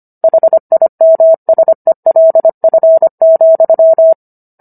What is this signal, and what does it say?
Morse code, heard as a keyed tone, about 25 wpm, 650 Hz. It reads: HIMSELF,